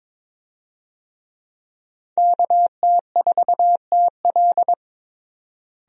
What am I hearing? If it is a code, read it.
Morse code, heard as a keyed tone, about 22 wpm, 700 Hz. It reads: KT4TL
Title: USA Call Signs - 5 Characters - Repeated 2x - 22wpm